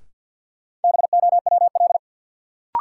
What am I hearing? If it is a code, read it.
Morse code, heard as a keyed tone, about 47 wpm, 700 Hz. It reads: BOWL
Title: Sentences from Top 1200 Words - New Word Drill - 50wpm